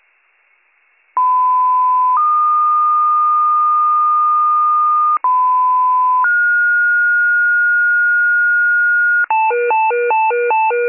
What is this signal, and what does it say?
Two-tone page, heard as a one-second tone, then a three-second tone, about 1010 then 1220 Hz, about 1010 then 1430 Hz, with noise. It is heard from 1.2 to 9.3 s.